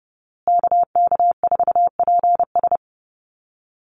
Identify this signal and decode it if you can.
Morse code, heard as a keyed tone, about 30 wpm, 700 Hz. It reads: KK4PS
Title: USA Call Signs Repeated 3x 30wpm